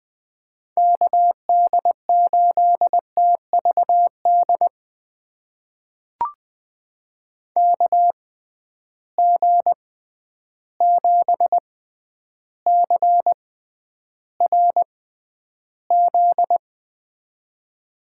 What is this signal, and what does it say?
Morse code, heard as a keyed tone, about 20 wpm, spaced as for 10 wpm, 700 Hz. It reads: KD8TVD KG7CRZ
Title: USA Call Signs - 6 Characters - 6x Character Spacing - 20wpm